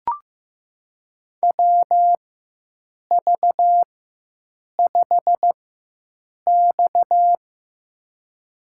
Morse code, heard as a keyed tone, about 15 wpm, spaced as for 7 wpm, 700 Hz. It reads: WV5X
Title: USA Call Signs - 4 Characters - 4x Character Spacing - 15wpm